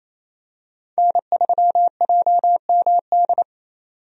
Morse code, heard as a keyed tone, about 28 wpm, 700 Hz. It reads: N3JMD
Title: USA Call Signs - 5 Characters - 28wpm